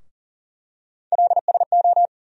Morse code, heard as a keyed tone, about 40 wpm, 700 Hz. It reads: LSO
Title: USA Call Sign Postfixes - 3 Letters 40wpm